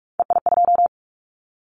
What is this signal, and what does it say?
Morse code, heard as a keyed tone, about 46 wpm, 700 Hz. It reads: EI2
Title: Callsign Prefixes for Top 50 DXCC - 45wpm